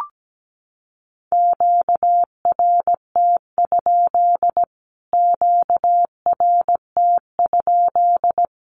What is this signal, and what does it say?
Morse code, heard as a keyed tone, about 17 wpm, 700 Hz. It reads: QRT? QRT?
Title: Q-Codes Repeated 3x 17wpm